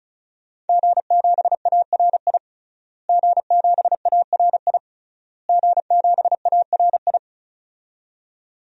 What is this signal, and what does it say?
Morse code, heard as a keyed tone, about 36 wpm, 700 Hz. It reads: G7ARI G7ARI G7ARI
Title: DXCC-100 Call Signs - 5 Characters - Repeated 3x - 35wpm